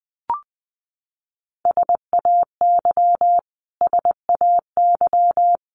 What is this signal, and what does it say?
Morse code, heard as a keyed tone, about 20 wpm, 700 Hz. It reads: SAY SAY
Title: Top 200 Words Repeated 20wpm